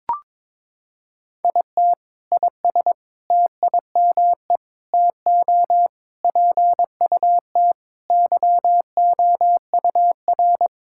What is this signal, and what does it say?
Morse code, heard as a keyed tone, about 22 wpm, 700 Hz. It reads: IT IS TIME TO PUT YOUR
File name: Sentences with 7-Word Phrases - 22wpm